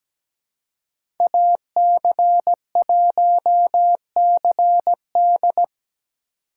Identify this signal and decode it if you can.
Morse code, heard as a keyed tone, about 17 wpm, 700 Hz. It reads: AC1CD
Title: USA Call Signs - 5 Characters - 3x Character Spacing - 17wpm